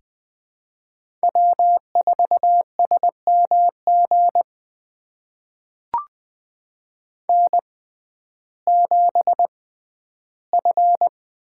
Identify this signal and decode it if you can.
Morse code, heard as a keyed tone, about 21 wpm, spaced as for 12 wpm, 700 Hz. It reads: W4SMG N7F
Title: USA Call Signs - 5 Characters - 6x Character Spacing - 20wpm